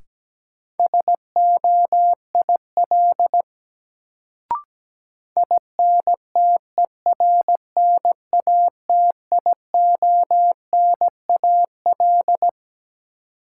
Morse code, heard as a keyed tone, about 17 wpm, 700 Hz. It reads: SOIL INTERNATIONAL
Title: Sentences from Top 1000 Words - Review Part 4 - New Word Drill - 17wpm